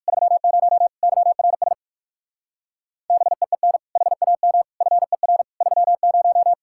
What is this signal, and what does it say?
Morse code, heard as a keyed tone, about 59 wpm, 700 Hz. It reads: 30 YRS BEEN HAM FER 30